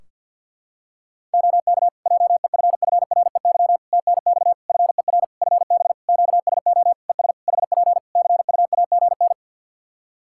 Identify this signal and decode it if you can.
Morse code, heard as a keyed tone, about 48 wpm, 700 Hz. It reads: OK JEFFREY TNX FER FB QSO ES HP CUAGN
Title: Practice QSO - Exchange 4 - Ending - 50wpm